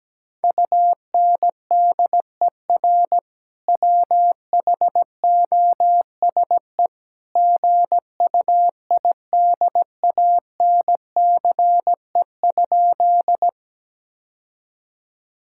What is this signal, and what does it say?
Morse code, heard as a keyed tone, about 17 wpm, 700 Hz. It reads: UNDER WHOSE GUIDANCE?
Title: Sentences from Top 1600 Words 17wpm